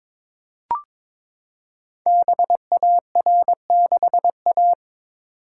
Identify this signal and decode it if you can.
Morse code, heard as a keyed tone, about 23 wpm, 690 Hz. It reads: BAR6A
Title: Code Groups - 5 Letter:Numbers - 22wpm